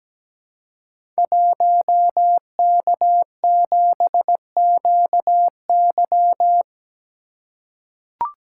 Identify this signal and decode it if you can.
Morse code, heard as a keyed tone, about 17 wpm, 700 Hz. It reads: 1K7QY
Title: Code Groups - 5 Letter:Numbers - 17wpm